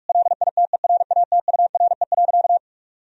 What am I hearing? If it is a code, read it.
Morse code, heard as a keyed tone, about 46 wpm, 700 Hz. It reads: LITERATURE.
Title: Sentences from Top 1000 Words - Review Part 3 - 45wpm